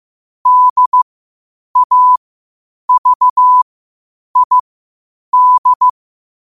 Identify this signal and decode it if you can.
Morse code, heard as a keyed tone, about 15 wpm, spaced as for 8 wpm, 1000 Hz. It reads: DAVID
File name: qotdfiles / 8wpm